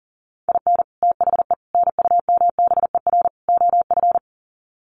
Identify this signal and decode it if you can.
Morse code, heard as a keyed tone, about 40 wpm, 700 Hz. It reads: IN THE NUMBER OF
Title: Sets of 4 Words 40wpm